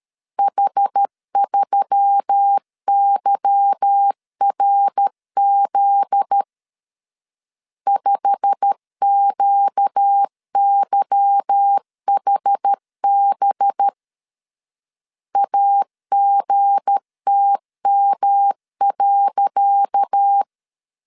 Morse code, heard as a keyed tone, about 12 wpm, 800 Hz. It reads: H3YRZ 5QYHB AGTM.
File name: LEZIONI / 10 SIGNS / 10C DOT 12wpm 2451